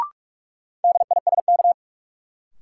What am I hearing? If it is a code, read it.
Morse code, heard as a keyed tone, about 46 wpm, 700 Hz. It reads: DISK
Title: Sentences from Top 1000 Words - Review Part 6 - New Word Drill - 45wpm